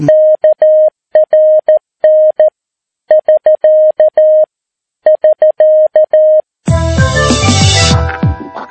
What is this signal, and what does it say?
Morse code, heard as a keyed tone, about 13 wpm, 620 Hz. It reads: KRN <SK> <SK>
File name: PI4AA Uitzending